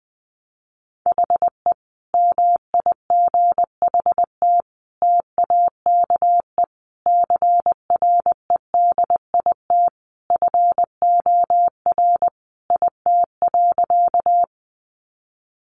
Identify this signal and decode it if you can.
Morse code, heard as a keyed tone, about 20 wpm, 700 Hz. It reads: HE MIGHT TAKE CREDIT FOR IT.
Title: Sentences from Top 1000 Words 20wpm